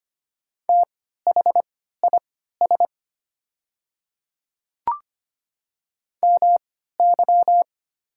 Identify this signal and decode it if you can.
Morse code, heard as a keyed tone, about 25 wpm, spaced as for 14 wpm, 690 Hz. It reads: THIS MY
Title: 3x Character Spacing - Top 200 Words - 25wpm